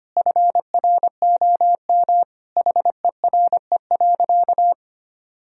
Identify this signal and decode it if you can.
Morse code, heard as a keyed tone, about 25 wpm, 700 Hz. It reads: FROM HERE.
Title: Sentences from Top 1000 Words - Review Part 4 - 25wpm